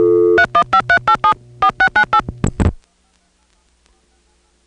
DTMF, heard as telephone touch tones, tones of about 80 ms. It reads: 31538 4698